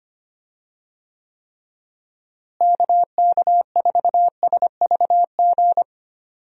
Morse code, heard as a keyed tone, about 25 wpm, 700 Hz. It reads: KK4SVG
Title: USA Call Signs - 6 Characters - Repeated 3x - 25wpm